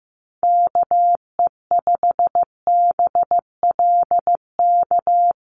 Morse code, heard as a keyed tone, about 15 wpm, 700 Hz. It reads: KE5BLK
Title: USA Call Signs Repeated 2x 15wpm